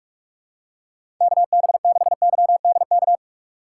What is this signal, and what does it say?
Morse code, heard as a keyed tone, about 46 wpm, 700 Hz. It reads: KB6YDK
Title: USA Call Signs - 6 Characters - 2x Character Spacing - 45wpm